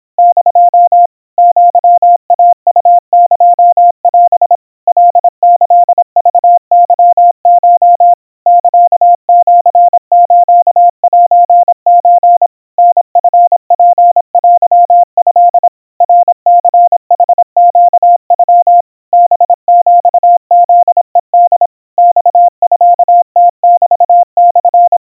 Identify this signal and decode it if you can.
Morse code (Japanese Wabun code), heard as a keyed tone, about 26 wpm, 700 Hz. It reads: <DO> アイウエオ カキクケコ サシスセソ タチツテト ナニヌネノ ハヒフヘホ マミムメモ
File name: aiueo 90 cpm 700 Hz Crispy